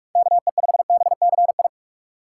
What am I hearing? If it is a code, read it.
Morse code, heard as a keyed tone, about 44 wpm, 700 Hz. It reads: KE5BCI
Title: USA Call Signs - 6 Characters - 8x Character Spacing - 45wpm